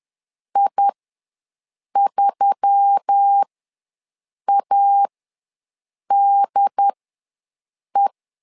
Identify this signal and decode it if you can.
Morse code, heard as a keyed tone, about 10 wpm, spaced as for 6 wpm, 800 Hz. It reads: I3ADE
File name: LEZIONI / 4 Q1S3 ws 10wpm 2437